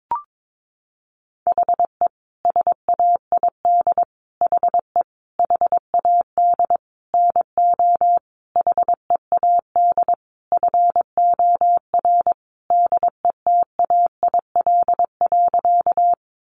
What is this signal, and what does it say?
Morse code, heard as a keyed tone, about 22 wpm, 700 Hz. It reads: HE SAID HE HAD NO HEAD FOR DETAIL.
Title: Sentences from Top 1500 Words 22wpm